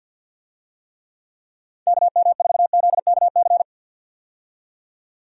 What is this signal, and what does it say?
Morse code, heard as a keyed tone, about 52 wpm, 700 Hz. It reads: KM4ZKC
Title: USA Call Signs - 6 Characters - 50wpm